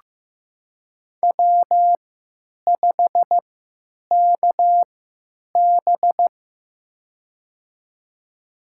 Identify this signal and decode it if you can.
Morse code, heard as a keyed tone, about 15 wpm, spaced as for 9 wpm, 700 Hz. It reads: W5KB
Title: USA Call Signs - 4 Characters - 3x Character Spacing - 15wpm